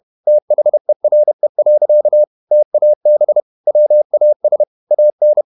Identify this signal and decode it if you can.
Morse code, heard as a keyed tone, about 31 wpm, 600 Hz. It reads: THERE. TAB WAS AN
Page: Never 31wpm | CW med Gnister